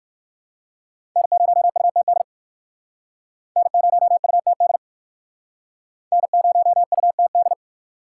Morse code, heard as a keyed tone, about 51 wpm, 690 Hz. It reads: N0UTD N0UTD N0UTD
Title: Speed Racing - USA 1x3 Call Signs 45wpm